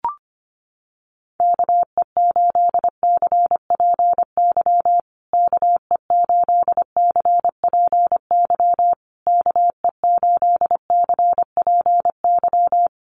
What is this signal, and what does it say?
Morse code, heard as a keyed tone, about 25 wpm, 700 Hz. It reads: KE8CPY KE8CPY KE8CPY
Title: USA Call Signs Repeated 3x 25wpm